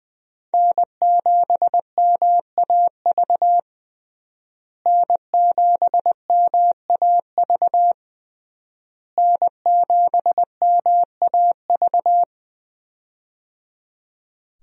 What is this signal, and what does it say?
Morse code, heard as a keyed tone, about 20 wpm, 700 Hz. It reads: N7MAV N7MAV N7MAV
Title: USA Call Signs - 5 Characters - Repeated 3x - 20wpm